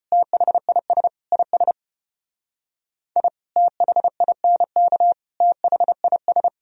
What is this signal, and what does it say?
Morse code, heard as a keyed tone, about 32 wpm, 700 Hz. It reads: THIS IS I THINK THIS